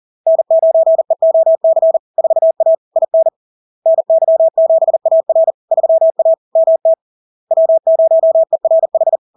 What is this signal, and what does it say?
Morse code, heard as a keyed tone, about 40 wpm, 650 Hz. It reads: N9EOC 4A IN NY7AR 3A MT W0ERH
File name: LARK-FD-2022-Log-in-CW-40wpm-650-Hz